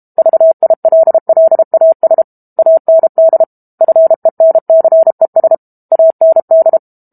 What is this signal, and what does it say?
Morse code, heard as a keyed tone, about 32 wpm, 650 Hz. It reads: VILLAS AND FENCES AND
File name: Book Two / 06-Chapter 6 - 32 WPM